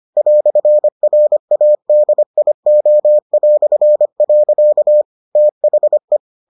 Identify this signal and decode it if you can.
Morse code, heard as a keyed tone, about 25 wpm, 600 Hz. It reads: "RADIO". THE